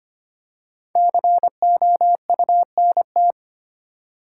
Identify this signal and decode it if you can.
Morse code, heard as a keyed tone, about 25 wpm, 700 Hz. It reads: COUNT
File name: 5 Letter Words - First 3 Letters Given - 25wpm